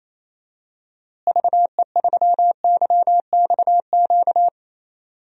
Morse code, heard as a keyed tone, about 28 wpm, 700 Hz. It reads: VE3YXQ
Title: DXCC-100 Call Signs - 6 Characters - Repeated 3x - 28wpm